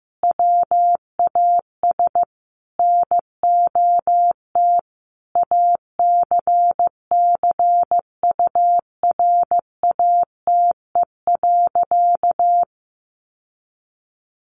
Morse code, heard as a keyed tone, about 15 wpm, 700 Hz. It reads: WAS NOT ACCURATE.
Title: Sentences from Top 1200 Words 15wpm